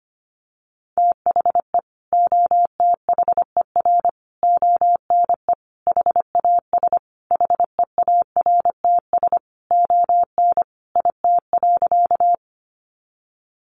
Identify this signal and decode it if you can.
Morse code, heard as a keyed tone, about 25 wpm, 700 Hz. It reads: THE OTHER ONE HAS HEARTS ON IT.
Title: Additional Easy Phrases 25wpm